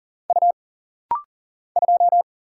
Morse code, heard as a keyed tone, about 40 wpm, 700 Hz. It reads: U 2
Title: Single Letter-Number Alt - Rapid-Fire 40wpm